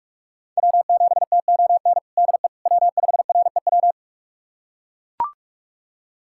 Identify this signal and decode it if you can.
Morse code, heard as a keyed tone, about 43 wpm, 700 Hz. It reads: W7TON DE W5REW
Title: USA Callsign DE Callsign - 5 Characters - 45wpm